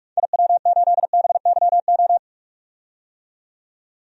Morse code, heard as a keyed tone, about 46 wpm, 700 Hz. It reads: IW8BYO
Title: DXCC-100 Call Signs - 6 Characters - 45wpm